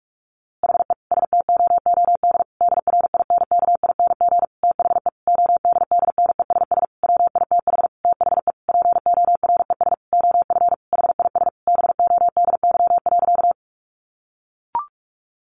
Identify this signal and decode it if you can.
Morse code, heard as a keyed tone, about 44 wpm, 700 Hz. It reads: HE STOOD DRINKING THE ODDNESS WITH THE PORES OF HIS BODY.
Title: A Sound Of Thunder 45wpm